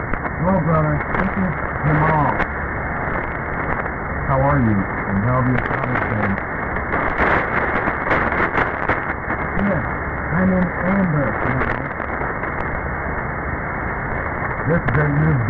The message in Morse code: none